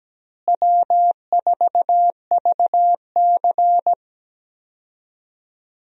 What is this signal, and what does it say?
Morse code, heard as a keyed tone, about 17 wpm, 700 Hz. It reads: W4VC
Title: USA Call Signs - 4 Characters - Repeated 3x - 17wpm